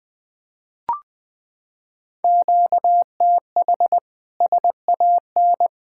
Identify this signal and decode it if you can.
Morse code, heard as a keyed tone, about 20 wpm, 700 Hz. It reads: QTH SAN
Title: Practice QSO - Exchange 1 - Beginning - 20wpm